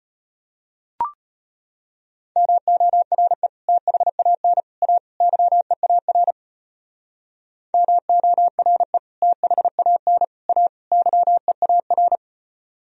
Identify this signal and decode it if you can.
Morse code, heard as a keyed tone, about 31 wpm, 700 Hz. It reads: MORE THAN A YEAR MORE THAN A YEAR
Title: Speed-Racing - Sets of 4 Words 25wpm